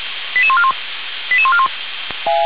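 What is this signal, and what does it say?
Five-tone selective call (ZVEI), heard as quick runs of five tones, tones of about 70 ms. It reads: 88141 88141
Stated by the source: ZVEI five-tone selective call received on an SDR for a fire brigade.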